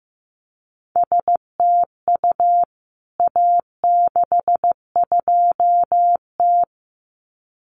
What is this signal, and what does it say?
Morse code, heard as a keyed tone, about 15 wpm, 700 Hz. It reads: STU A62T
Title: CWT Exchange 15wpm